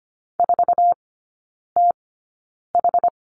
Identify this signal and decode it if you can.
Morse code, heard as a keyed tone, about 25 wpm, spaced as for 9 wpm, 700 Hz. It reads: 4TH
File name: Lesson 15e - Number 5 - Three Characters 25wpm